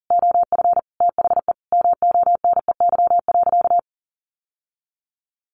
Morse code, heard as a keyed tone, about 40 wpm, 700 Hz. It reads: OF THE MONEY.
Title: Sentences from Top 1100 Words 40wpm